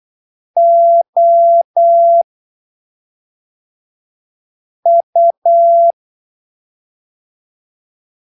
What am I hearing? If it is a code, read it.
Morse code, heard as a keyed tone, about 8 wpm, spaced as for 3 wpm, 680 Hz. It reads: OU
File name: arquivos / texto / texto 11 08ppm